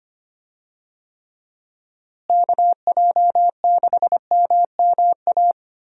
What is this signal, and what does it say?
Morse code, heard as a keyed tone, about 25 wpm, 700 Hz. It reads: KJ6MMA
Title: USA Call Signs - 6 Characters - Repeated 3x - 25wpm